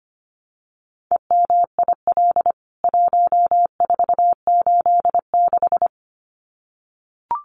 Morse code, heard as a keyed tone, about 25 wpm, 700 Hz. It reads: EMIL 1486
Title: CWT Exchange 25wpm